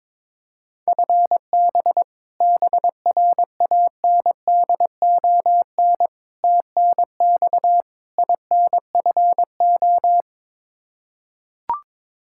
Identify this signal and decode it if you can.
Morse code, heard as a keyed tone, about 22 wpm, 700 Hz. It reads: FB BRANDON TNX INFO
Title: Practice QSO - Exchange 3 - More Info - 22wpm